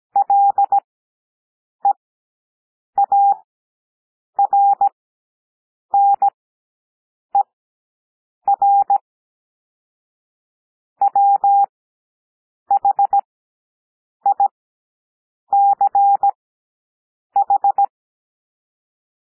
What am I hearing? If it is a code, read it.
Morse code, heard as a keyed tone, about 17 wpm, spaced as for 7 wpm, 800 Hz. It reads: LEARNER WHICH
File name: SOUND PLAIN 17 wpm